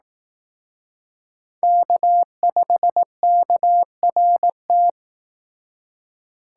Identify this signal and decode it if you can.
Morse code, heard as a keyed tone, about 18 wpm, 700 Hz. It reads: K5KRT
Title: Speed Racing - USA 1x3 Call Signs 15wpm